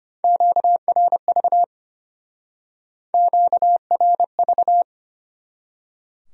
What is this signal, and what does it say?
Morse code, heard as a keyed tone, about 33 wpm, 700 Hz. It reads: QRV QRV